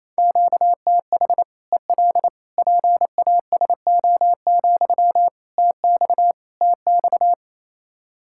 Morse code, heard as a keyed tone, about 28 wpm, 700 Hz. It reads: QTH EL PASO, TX TX
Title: Practice QSO - Exchange 1 - Beginning - 28wpm